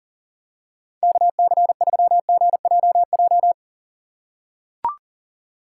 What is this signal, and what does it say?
Morse code, heard as a keyed tone, about 40 wpm, 700 Hz. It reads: KC3GJJ
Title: USA Call Signs - 6 Characters - 3x Character Spacing - 40wpm